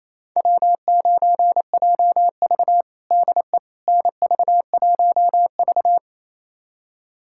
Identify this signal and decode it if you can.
Morse code, heard as a keyed tone, about 28 wpm, 700 Hz. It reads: W9JV DE NV1V
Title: USA Callsign DE Callsign - 4 Characters - 28wpm